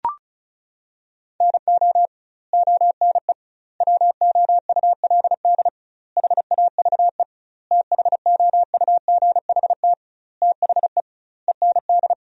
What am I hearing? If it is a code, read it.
Morse code, heard as a keyed tone, about 36 wpm, 700 Hz. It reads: NO ONE WOULD HAVE THOUGHT THE END
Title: 2x Word-Spacing - Sentences from Top 300 Words - 35wpm